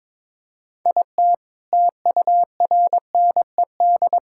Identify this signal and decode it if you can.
Morse code, heard as a keyed tone, about 22 wpm, 700 Hz. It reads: IT TURNED
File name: practice / 22wpm / Build A Phrase - 5-Word Phrases - 22wpm